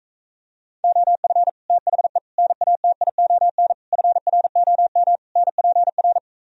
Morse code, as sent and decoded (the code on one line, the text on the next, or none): --- ..-. / - .... / -. .- - .. --- -. / ..-. .-. --- -- / -. .--. .-.
OF THE NATION FROM NPR